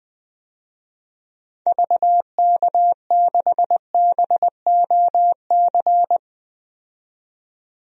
Morse code, heard as a keyed tone, about 21 wpm, 700 Hz. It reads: VK6BOC
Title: DXCC-100 Call Signs - 6 Characters - 20wpm